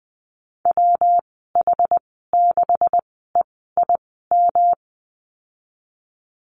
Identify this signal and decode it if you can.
Morse code, heard as a keyed tone, about 20 wpm, spaced as for 15 wpm, 700 Hz. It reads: WH6EIM
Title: 179 Lesson 29g - Slash - Call Signs 20wpm